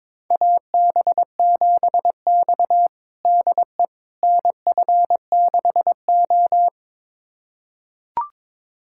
Morse code, heard as a keyed tone, about 22 wpm, 700 Hz. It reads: AB7X DE NF6O
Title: USA Callsign DE Callsign - 4 Characters - 22wpm